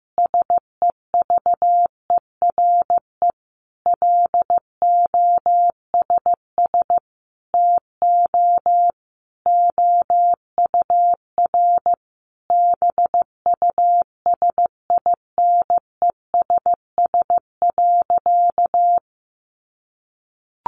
Morse code, heard as a keyed tone, about 15 wpm, 700 Hz. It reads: SEVERE LOSS TO OUR BUSINESS.
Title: Sentences from Top 1400 Words 15wpm